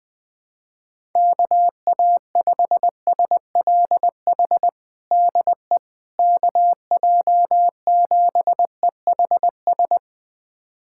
Morse code, heard as a keyed tone, about 20 wpm, 700 Hz. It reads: KA5SLH DE KJ7EHS
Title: USA Callsign DE Callsign 20wpm